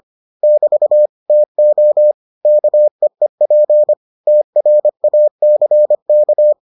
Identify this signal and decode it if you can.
Morse code, heard as a keyed tone, about 25 wpm, 600 Hz. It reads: <BT> TO KEEP TRACK